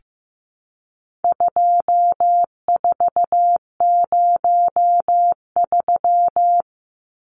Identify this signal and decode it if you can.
Morse code, heard as a keyed tone, about 15 wpm, 700 Hz. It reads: 2403
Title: Numbers - 4-Digit 15wpm